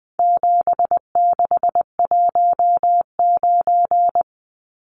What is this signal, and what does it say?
Morse code, heard as a keyed tone, about 20 wpm, 700 Hz. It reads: 7619
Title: Numbers - 4-Digit 20wpm